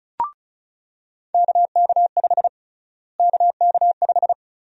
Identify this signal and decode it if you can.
Morse code, heard as a keyed tone, about 34 wpm, 700 Hz. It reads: KK5 KK5
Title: USA Call Sign Prefixes - Repeated 2x - 35wpm